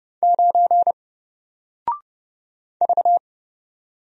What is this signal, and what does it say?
Morse code, heard as a keyed tone, about 31 wpm, 700 Hz. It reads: 9 V
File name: Single-Letter-Number-30wpm